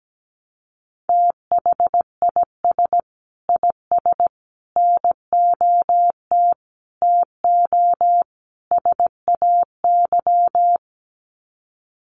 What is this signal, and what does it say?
Morse code, heard as a keyed tone, about 17 wpm, 700 Hz. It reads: THIS IS NOT TO SAY
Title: Sets of 5 Words Encore 17wpm